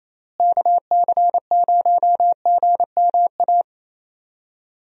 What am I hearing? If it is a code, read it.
Morse code, heard as a keyed tone, about 28 wpm, 700 Hz. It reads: KC0GMA
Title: USA Call Signs - 6 Characters - Repeated 2x - 28wpm